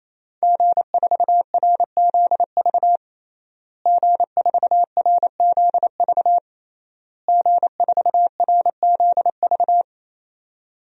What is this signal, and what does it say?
Morse code, heard as a keyed tone, about 28 wpm, 690 Hz. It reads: G4RZV G4RZV G4RZV
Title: DXCC-100 Call Signs - 5 Characters - Repeated 3x - 28wpm